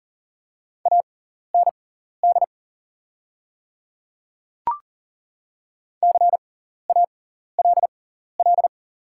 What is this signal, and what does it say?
Morse code, heard as a keyed tone, about 40 wpm, spaced as for 14 wpm, 700 Hz. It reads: AND CALL
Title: 6x Character Spacing - Top 100 Words - 40wpm